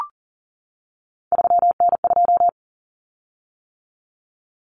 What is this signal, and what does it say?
Morse code, heard as a keyed tone, about 40 wpm, 700 Hz. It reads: 3N2
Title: Common Cut Numbers - 3-Digit 40wpm